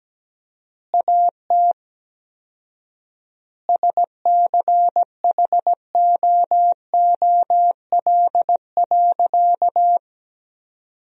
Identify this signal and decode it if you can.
Morse code, heard as a keyed tone, about 17 wpm, 700 Hz. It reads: AT SCHOOL.
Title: 4x Word-Spacing - Sentences from Top 300 Words - 17wpm